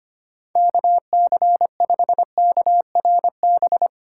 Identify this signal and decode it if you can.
Morse code, heard as a keyed tone, about 25 wpm, 700 Hz. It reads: KC5KRB
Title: USA Call Signs - 3x Character Spacing - 25wpm